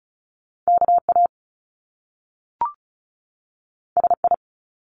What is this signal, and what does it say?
Morse code, heard as a keyed tone, about 36 wpm, 700 Hz. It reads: KA SI
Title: Top Two Letter Combinations in Words 35wpm